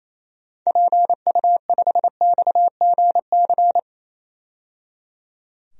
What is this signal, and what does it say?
Morse code, heard as a keyed tone, about 28 wpm, 700 Hz. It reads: PU5XGC
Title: DXCC-100 Call Signs - 6 Characters - 28wpm